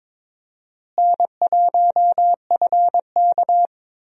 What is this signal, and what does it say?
Morse code, heard as a keyed tone, about 22 wpm, 700 Hz. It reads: N1FK
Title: USA Call Signs - 4 Characters - Repeated 3x - 22wpm